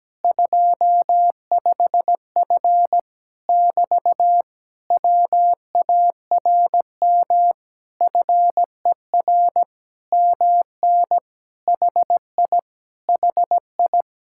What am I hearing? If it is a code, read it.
Morse code, heard as a keyed tone, about 17 wpm, 700 Hz. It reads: 25F <BT> WARM FER MN HI HI
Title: Speed-Racing Common QSO Phrases Encore - 17wpm